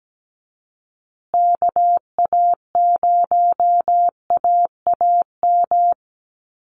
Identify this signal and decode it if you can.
Morse code, heard as a keyed tone, about 17 wpm, 700 Hz. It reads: KA0AAM
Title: Call Signs Repeated 3x - 17wpm